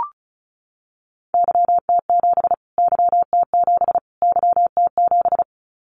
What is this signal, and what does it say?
Morse code, heard as a keyed tone, about 34 wpm, 700 Hz. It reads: YT7 YT7 YT7
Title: Callsign Prefixes for Top 25 DXCC - Repeated 3x - 35wpm